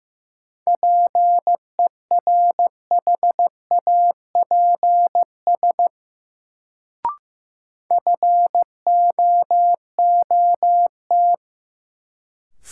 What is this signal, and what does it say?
Morse code, heard as a keyed tone, about 15 wpm, 700 Hz. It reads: PERHAPS FOOT
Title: Sentences from Top 500 Words - New Word Drill - 15wpm